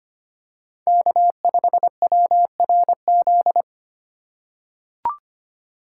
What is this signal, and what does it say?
Morse code, heard as a keyed tone, about 25 wpm, 700 Hz. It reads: K5WRZ
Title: USA Call Signs - 5 Characters - 3x Character Spacing - 25wpm